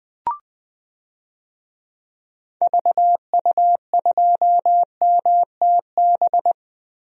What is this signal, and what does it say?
Morse code, heard as a keyed tone, about 20 wpm, 700 Hz. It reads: VU2MTB
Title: DXCC-100 Call Signs - 6 Characters - Repeated 3x - 20wpm